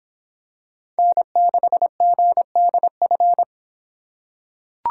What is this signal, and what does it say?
Morse code, heard as a keyed tone, about 25 wpm, 700 Hz. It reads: N6GDF